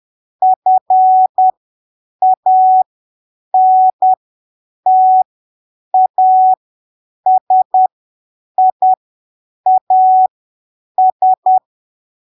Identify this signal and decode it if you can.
Morse code, heard as a keyed tone, about 10 wpm, spaced as for 7 wpm, 750 Hz. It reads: FANTASIAS
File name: arquivos / texto / texto 26 10ppm